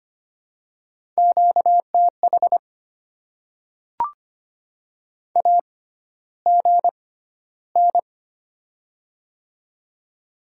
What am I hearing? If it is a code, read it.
Morse code, heard as a keyed tone, about 25 wpm, spaced as for 13 wpm, 700 Hz. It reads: QTH AGN
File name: Top 20 QSO Elements - 6x Character Spacing - 25wpm